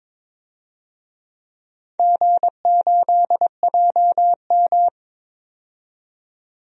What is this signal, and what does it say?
Morse code, heard as a keyed tone, about 22 wpm, 700 Hz. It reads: G8JM
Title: DXCC-100 Call Signs - 4 Characters - 22wpm